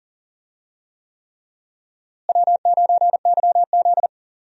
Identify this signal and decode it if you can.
Morse code, heard as a keyed tone, about 40 wpm, 700 Hz. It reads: W9YZ
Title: USA Call Signs - 4 Characters - Repeated 3x - 40wpm